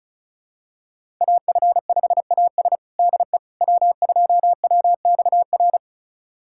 Morse code, heard as a keyed tone, about 36 wpm, 700 Hz. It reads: AF5AS DE W2WXR
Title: USA Callsign DE Callsign - 5 Characters - 35wpm